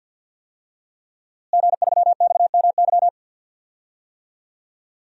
Morse code, heard as a keyed tone, about 52 wpm, 700 Hz. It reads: G3XMY
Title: DXCC-100 Call Signs - 5 Characters - Repeated 2x - 50wpm